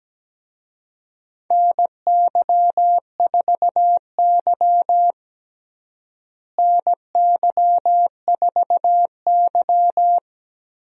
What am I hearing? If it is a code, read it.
Morse code, heard as a keyed tone, about 17 wpm, 700 Hz. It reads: NY4Y NY4Y
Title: USA Call Signs - 4 Characters - Repeated 3x - 17wpm